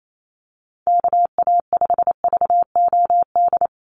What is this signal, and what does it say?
Morse code, heard as a keyed tone, about 28 wpm, 700 Hz. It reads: KA5VOD
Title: USA Call Signs 28wpm